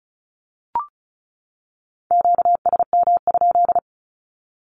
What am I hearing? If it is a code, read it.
Morse code, heard as a keyed tone, about 34 wpm, 700 Hz. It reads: QSM?
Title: Q-Codes 35wpm